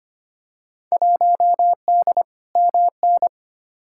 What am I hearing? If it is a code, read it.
Morse code, heard as a keyed tone, about 25 wpm, 700 Hz. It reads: 1D MN
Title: Field Day Exchanges - 25wpm